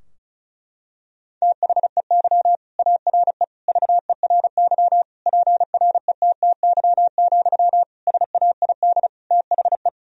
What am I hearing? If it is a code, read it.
Morse code, heard as a keyed tone, about 34 wpm, 700 Hz. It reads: THEY ARE VERY PRETTY, SAID THE
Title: Sentences from Top 1000 Words - Review Part 4 - 35wpm